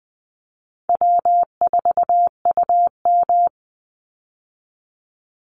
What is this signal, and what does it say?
Morse code, heard as a keyed tone, about 20 wpm, 700 Hz. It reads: W4UM
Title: Easy Call Signs Repeated 2x - 20wpm